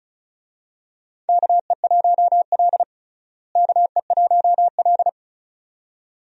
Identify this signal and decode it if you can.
Morse code, heard as a keyed tone, about 36 wpm, 700 Hz. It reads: KE1L KE1L
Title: USA Call Signs - 4 Characters - Repeated 2x - 35wpm